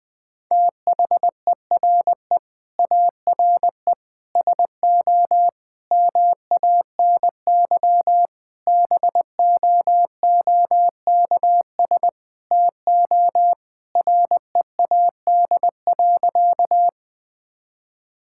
Morse code, heard as a keyed tone, about 20 wpm, 700 Hz. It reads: THERE ARE SO MANY BOOKS TO READ.
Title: Sentences with 4-Word Phrases - 20wpm